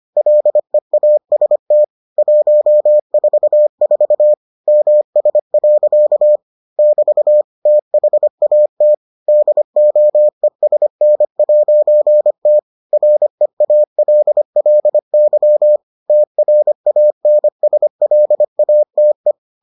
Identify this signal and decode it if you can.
Morse code, heard as a keyed tone, about 25 wpm, 600 Hz. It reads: LEAST 144 MS. <BT> THAT DOESN'T REALLY TRANSLATE